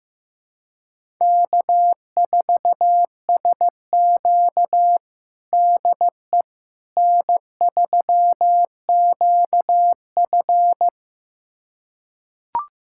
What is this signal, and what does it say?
Morse code, heard as a keyed tone, about 15 wpm, 700 Hz. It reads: K4SQ DE N3QF
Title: USA Callsign DE Callsign - 4 Characters - 15wpm